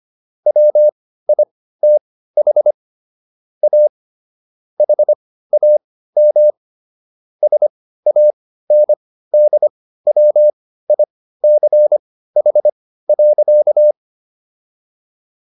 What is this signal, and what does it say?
Morse code, heard as a keyed tone, about 25 wpm, spaced as for 15 wpm, 600 Hz. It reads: WITH A HAM SANDWICH.